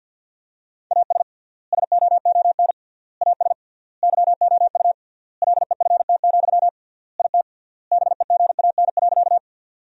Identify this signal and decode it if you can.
Morse code, heard as a keyed tone, about 50 wpm, 700 Hz. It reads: AS SOON AS YOU LEFT, IT BEGAN.
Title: 3x Word-Spacing - Sentences from Top 400 Words - 50wpm